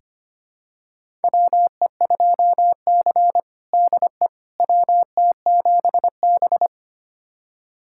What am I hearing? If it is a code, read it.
Morse code, heard as a keyed tone, about 25 wpm, 700 Hz. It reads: WE2C DE WT7B
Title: USA Callsign DE Callsign - 4 Characters - 25wpm